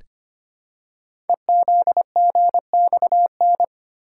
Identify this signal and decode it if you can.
Morse code, heard as a keyed tone, about 25 wpm, 700 Hz. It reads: EZGXN
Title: Code Groups - 5 Letter:Numbers - 25wpm